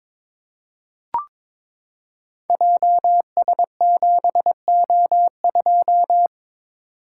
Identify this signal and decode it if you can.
Morse code, heard as a keyed tone, about 22 wpm, 700 Hz. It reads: JS7O2
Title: Code Groups - 5 Letter:Numbers - 22wpm